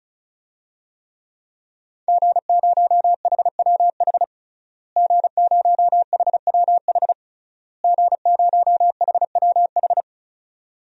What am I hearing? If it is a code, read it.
Morse code, heard as a keyed tone, about 34 wpm, 700 Hz. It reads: G0HWH G0HWH G0HWH
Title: DXCC-100 Call Signs - 5 Characters - Repeated 3x - 35wpm